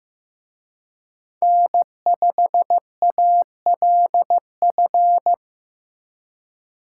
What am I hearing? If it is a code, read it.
Morse code, heard as a keyed tone, about 18 wpm, 700 Hz. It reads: N5ALF
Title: Speed Racing - USA 1x3 Call Signs 15wpm